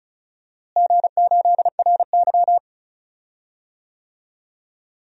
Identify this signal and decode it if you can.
Morse code, heard as a keyed tone, about 36 wpm, 700 Hz. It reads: G8RY
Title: DXCC-100 Call Signs - 4 Characters - Repeated 2x - 35wpm